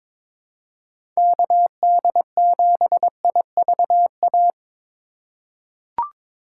Morse code, heard as a keyed tone, about 22 wpm, 700 Hz. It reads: KD7IVA
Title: USA Call Signs - 6 Characters - 6x Character Spacing - 22wpm